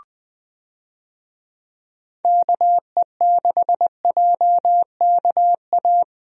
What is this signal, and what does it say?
Morse code, heard as a keyed tone, about 21 wpm, 700 Hz. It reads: KE6JKA
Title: USA Call Signs - 6 Characters - Repeated 2x - 20wpm